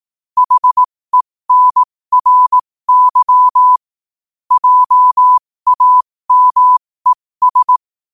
Morse code, heard as a keyed tone, about 18 wpm, spaced as for 15 wpm, 1000 Hz. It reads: HENRY JAMES
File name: qotdfiles / 15wpm